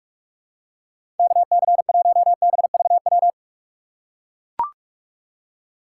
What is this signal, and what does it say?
Morse code, heard as a keyed tone, about 43 wpm, 700 Hz. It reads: KC1BVW
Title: USA Call Signs - 6 Characters - 3x Character Spacing - 45wpm